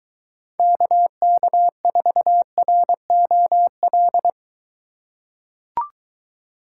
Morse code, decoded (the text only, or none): KK4ROL